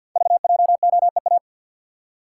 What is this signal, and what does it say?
Morse code, heard as a keyed tone, about 52 wpm, 700 Hz. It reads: VJOEA
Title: Code Groups - 5 Letters - 50wpm